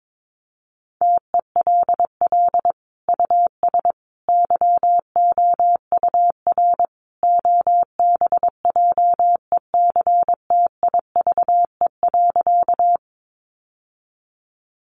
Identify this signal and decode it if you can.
Morse code, heard as a keyed tone, about 22 wpm, 700 Hz. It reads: TELL US YOUR OBJECTIVE.
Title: Sentences from Top 1200 Words 22wpm